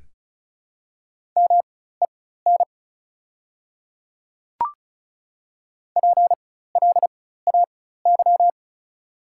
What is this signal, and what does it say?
Morse code, heard as a keyed tone, about 36 wpm, spaced as for 16 wpm, 700 Hz. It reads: MEN PLAY
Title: 4x Character Spacing - Top 200 Words - 35wpm